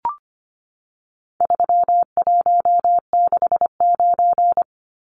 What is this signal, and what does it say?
Morse code, heard as a keyed tone, about 25 wpm, 700 Hz. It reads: 3169
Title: Numbers - 4-Digit 25wpm